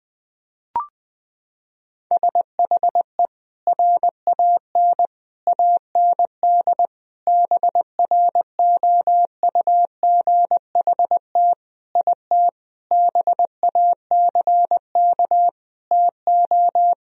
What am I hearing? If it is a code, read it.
Morse code, heard as a keyed tone, about 20 wpm, 700 Hz. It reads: SHE RAN AND BROUGHT IT BACK TO
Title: Sentences from Top 1000 Words - Review Part 3 - 20wpm